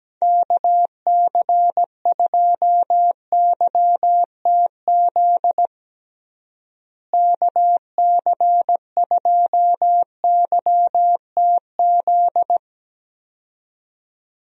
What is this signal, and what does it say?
Morse code, heard as a keyed tone, about 17 wpm, 700 Hz. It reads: KC2YTZ KC2YTZ
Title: USA Call Signs - 6 Characters - Repeated 2x - 17wpm